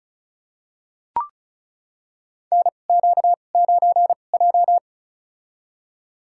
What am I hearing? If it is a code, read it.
Morse code, heard as a keyed tone, about 34 wpm, spaced as for 25 wpm, 700 Hz. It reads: NQ9J
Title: USA Call Signs - 2x Character Spacing - 35wpm